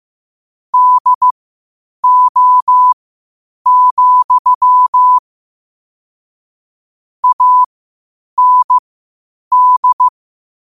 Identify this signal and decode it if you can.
Morse code, heard as a keyed tone, about 15 wpm, spaced as for 8 wpm, 1000 Hz. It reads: DO, AND